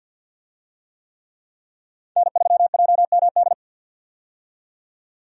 Morse code, heard as a keyed tone, about 53 wpm, 700 Hz. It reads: N3JMD
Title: USA Call Signs - 5 Characters - 50wpm